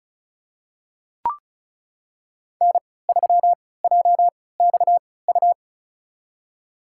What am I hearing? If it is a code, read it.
Morse code, heard as a keyed tone, about 36 wpm, spaced as for 20 wpm, 690 Hz. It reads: N3JXU
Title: USA Call Signs - 5 Characters - 3x Character Spacing - 35wpm